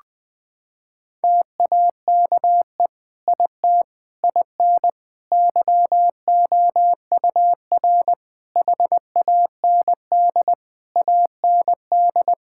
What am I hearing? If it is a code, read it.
Morse code, heard as a keyed tone, about 21 wpm, 700 Hz. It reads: TAKE IT IN YOUR HAND AND
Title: Sentences from Top 1600 Words 20wpm